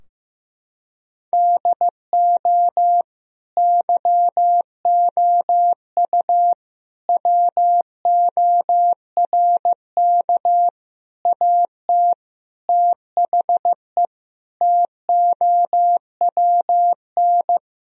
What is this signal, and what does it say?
Morse code, heard as a keyed tone, about 16 wpm, 700 Hz. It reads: DO YOU WORK AT THE TOWN
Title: Sentences from Top 1000 Words - Review Part 2 - 15wpm